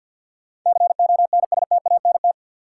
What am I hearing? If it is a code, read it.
Morse code, heard as a keyed tone, about 52 wpm, 700 Hz. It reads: CONSTANT
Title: Sentences from Top 1500 Words - New Word Drill - 50wpm